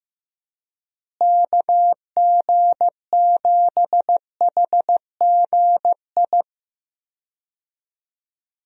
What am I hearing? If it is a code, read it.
Morse code, heard as a keyed tone, about 15 wpm, 700 Hz. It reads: KG7HGI
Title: USA Call Signs - 6 Characters - Repeated 2x - 15wpm